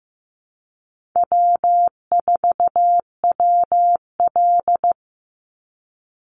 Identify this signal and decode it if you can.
Morse code, heard as a keyed tone, about 15 wpm, 700 Hz. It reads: W4WL
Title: USA Call Signs 15wpm